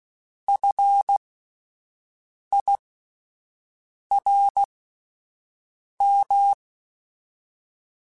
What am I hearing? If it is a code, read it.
Morse code, heard as a keyed tone, about 16 wpm, spaced as for 5 wpm, 800 Hz. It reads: FIRM
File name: Alphabetical / Alphabetical mp3s / Fs @5